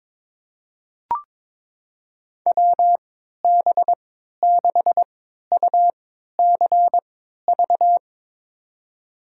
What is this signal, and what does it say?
Morse code, heard as a keyed tone, about 22 wpm, spaced as for 13 wpm, 700 Hz. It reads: WB6UCV
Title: USA Call Signs - 6 Characters - 3x Character Spacing - 22wpm